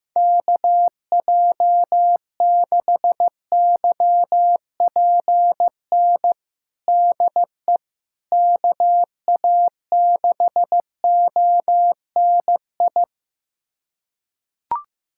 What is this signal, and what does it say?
Morse code, heard as a keyed tone, about 15 wpm, 700 Hz. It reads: KJ6YPN DE KA6ONI
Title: USA Callsign DE Callsign - 6 Characters - 15wpm